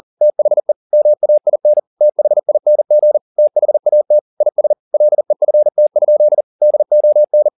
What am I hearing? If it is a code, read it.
Morse code, heard as a keyed tone, about 40 wpm, 600 Hz. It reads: THE MAIN THING THAT IS LEFT? DON